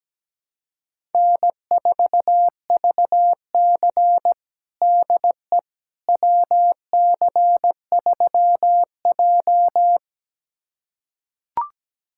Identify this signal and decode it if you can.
Morse code, heard as a keyed tone, about 17 wpm, 700 Hz. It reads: N4VC DE WC3J
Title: USA Callsign DE Callsign - 4 Characters - 17wpm